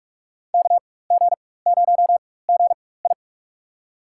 Morse code, heard as a keyed tone, about 47 wpm, spaced as for 21 wpm, 700 Hz. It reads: KG0GI
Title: USA Call Signs - 5 Characters - 4x Character Spacing - 45wpm